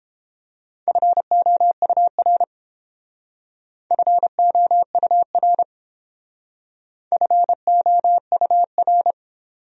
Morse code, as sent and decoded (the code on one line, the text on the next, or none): ..-. --- ..- .-. / ..-. --- ..- .-. / ..-. --- ..- .-.
FOUR FOUR FOUR